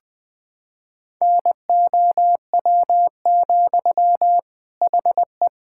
Morse code, heard as a keyed tone, about 20 wpm, 700 Hz. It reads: NOW, HE
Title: Sentences from Top 1600 Words 20wpm